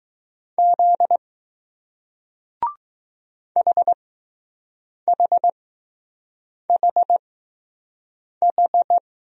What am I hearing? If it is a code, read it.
Morse code, heard as a keyed tone, about 20 wpm, spaced as for 6 wpm, 700 Hz. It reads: Z HHHH